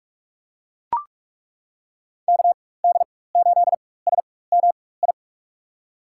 Morse code, heard as a keyed tone, about 41 wpm, spaced as for 20 wpm, 700 Hz. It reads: KD8SMI
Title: USA Call Signs - 6 Characters - 4x Character Spacing - 45wpm